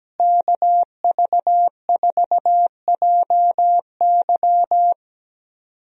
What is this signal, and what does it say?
Morse code, heard as a keyed tone, about 17 wpm, 700 Hz. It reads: KV4JY
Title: USA Call Signs - 5 Characters - Repeated 3x - 17wpm